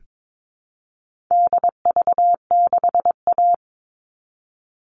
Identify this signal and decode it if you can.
Morse code, heard as a keyed tone, about 22 wpm, 700 Hz. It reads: DV6A
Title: All Cut Numbers - 4-Digit 22wpm